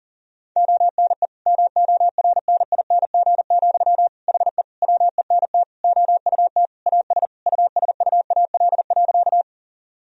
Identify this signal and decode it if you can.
Morse code, heard as a keyed tone, about 37 wpm, 700 Hz. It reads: ONE MORNING, HE WENT OUT AS USUAL.
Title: Sentences from Top 1000 Words - Review Part 2 - 40wpm